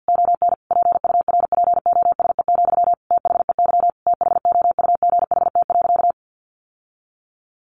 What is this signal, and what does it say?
Morse code, heard as a keyed tone, about 50 wpm, 700 Hz. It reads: ON PURPOSE, THEY THOUGHT.
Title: Sentences from Top 800 Words 50wpm